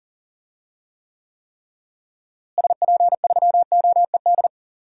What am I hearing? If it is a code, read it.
Morse code, heard as a keyed tone, about 39 wpm, 700 Hz. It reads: SP3OED
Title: DXCC-100 Call Signs - 6 Characters - Repeated 3x - 40wpm